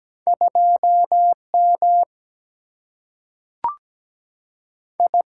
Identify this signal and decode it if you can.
Morse code, heard as a keyed tone, about 17 wpm, 700 Hz. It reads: 2 M I
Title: Single-Letter Single-Number 17wpm